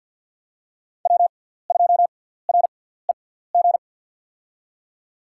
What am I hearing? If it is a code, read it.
Morse code, heard as a keyed tone, about 51 wpm, spaced as for 17 wpm, 700 Hz. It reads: W2REG
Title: USA Call Signs - 5 Characters - 6x Character Spacing - 50wpm